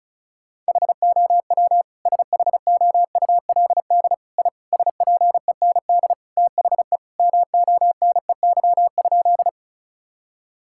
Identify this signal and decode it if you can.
Morse code, heard as a keyed tone, about 36 wpm, 700 Hz. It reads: HOW SHOULD I SPEND THE MONEY?
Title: Sentences from Top 1000 Words - Review Part 5 - 35wpm